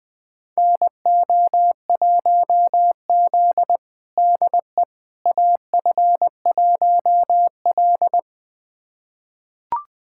Morse code, heard as a keyed tone, about 21 wpm, 700 Hz. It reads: NO1Z DE AF1L
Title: USA Callsign DE Callsign - 4 Characters - 20wpm